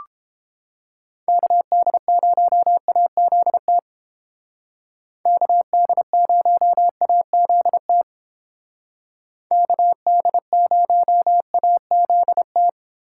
Morse code, heard as a keyed tone, about 29 wpm, 700 Hz. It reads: KD0AZT KD0AZT KD0AZT